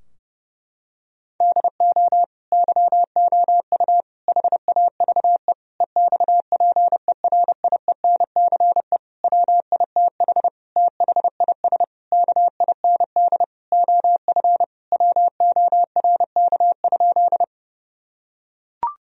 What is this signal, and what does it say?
Morse code, heard as a keyed tone, about 30 wpm, 700 Hz. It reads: DO YOU HAVE EXPERIENCE WITH THIS KIND OF WORK?
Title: Sentences from Top 1000 Words - Review Part 3 - 30wpm